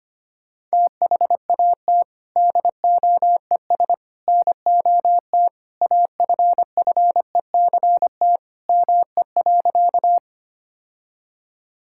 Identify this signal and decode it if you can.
Morse code, heard as a keyed tone, about 25 wpm, 700 Hz. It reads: THAT DOES NOT AFFECT ME.
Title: Sentences from Top 1000 Words - Review Part 6 - 25wpm